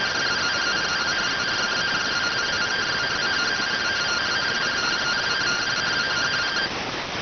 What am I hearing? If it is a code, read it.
Two-tone teleprinter signal, about 170 Hz shift.